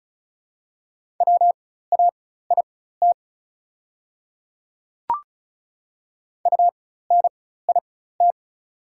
Morse code, heard as a keyed tone, about 36 wpm, spaced as for 16 wpm, 700 Hz. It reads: WAIT UNIT
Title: 4x Character Spacing - Top 500 Words - 35wpm